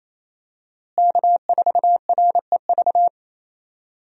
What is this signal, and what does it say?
Morse code, heard as a keyed tone, about 27 wpm, 700 Hz. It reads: K4REV
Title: USA Call Signs - 5 Characters - Repeated 2x - 28wpm